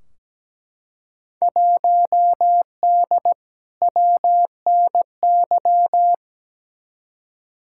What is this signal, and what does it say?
Morse code, heard as a keyed tone, about 17 wpm, 700 Hz. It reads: 1D WNY
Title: Field Day Exchanges - 17wpm